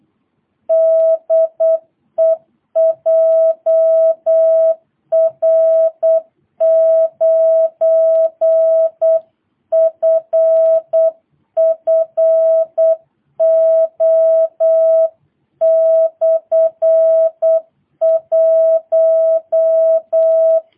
Morse code, heard as a keyed tone, about 8 wpm, 660 Hz. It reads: DE JR9FFO/1
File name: jh8cbh / 2022cw